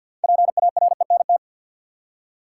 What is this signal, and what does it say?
Morse code, heard as a keyed tone, about 45 wpm, 700 Hz. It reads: PARENT
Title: Six Letter Words - 50wpm